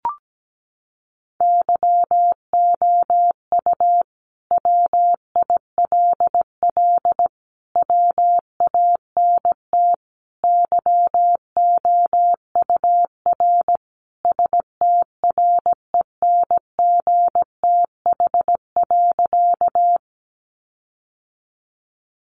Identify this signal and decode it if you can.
Morse code, heard as a keyed tone, about 17 wpm, 700 Hz. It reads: YOU WILL WANT YOUR STRENGTH.
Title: Sentences from Top 1300 Words 17wpm